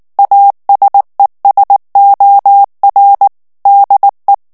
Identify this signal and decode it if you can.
Morse code, heard as a keyed tone, about 19 wpm, 800 Hz. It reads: ASESOR DE
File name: morse mp3 / ESPAGNOL TEXTE EN CLAIR 2 VITESSE 18 MOTS MINUTES